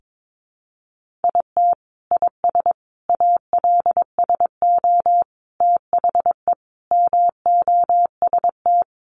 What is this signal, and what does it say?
Morse code, heard as a keyed tone, about 22 wpm, 700 Hz. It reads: IT IS ALSO THE MOST
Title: Sentences from Top 1400 Words 22wpm